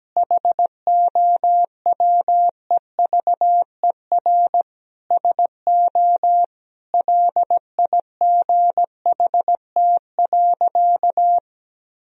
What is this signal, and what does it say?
Morse code, heard as a keyed tone, about 17 wpm, 700 Hz. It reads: HOWEVER SO LIGHT.
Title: Sentences from Top 1000 Words - Review Part 2 - 17wpm